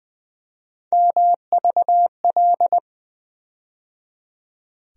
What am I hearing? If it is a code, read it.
Morse code, heard as a keyed tone, about 20 wpm, 700 Hz. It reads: MVL
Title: USA Call Sign Postfixes - 3 Letters 20wpm